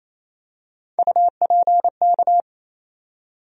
Morse code, heard as a keyed tone, about 28 wpm, 700 Hz. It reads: UPK
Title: USA Call Sign Postfixes - 3 Letters 28wpm